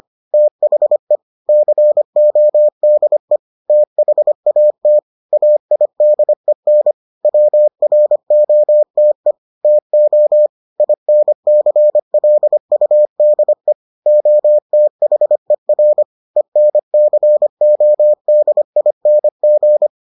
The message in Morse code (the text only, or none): THE CODE THAT AIDEN WROTE TO INCLUDE OTHER ENCODING